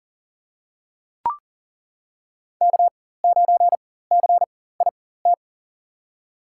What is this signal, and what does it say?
Morse code, heard as a keyed tone, about 40 wpm, spaced as for 19 wpm, 700 Hz. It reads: K9CIT
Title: USA Call Signs - 5 Characters - 4x Character Spacing - 40wpm